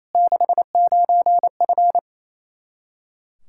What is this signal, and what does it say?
Morse code, heard as a keyed tone, about 28 wpm, 700 Hz. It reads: 69F
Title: K1USN SST QSO Practice 28wpm